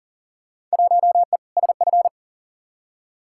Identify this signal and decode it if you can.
Morse code, heard as a keyed tone, about 40 wpm, 700 Hz. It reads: 1E SF
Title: Field Day Exchanges - 40wpm